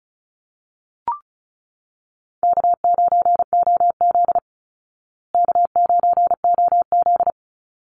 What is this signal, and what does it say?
Morse code, heard as a keyed tone, about 34 wpm, 700 Hz. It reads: K9OZ K9OZ
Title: Call Signs Repeated 3x - 35wpm